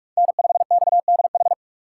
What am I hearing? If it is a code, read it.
Morse code, heard as a keyed tone, about 40 wpm, 700 Hz. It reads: N5XDH